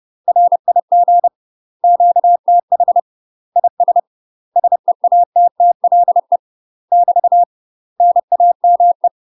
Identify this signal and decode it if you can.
Morse code, heard as a keyed tone, about 30 wpm, 700 Hz. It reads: RIG QTH IS SEATTLE <BT> NAME
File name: UK MASTER QSOs TRAINING / 30WPM / UK MASTER QSOs 0040